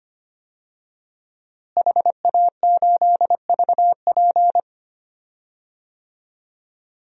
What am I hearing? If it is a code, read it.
Morse code, heard as a keyed tone, about 25 wpm, 700 Hz. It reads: HA8VP
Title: DXCC-100 Call Signs - 5 Characters - Repeated 3x - 25wpm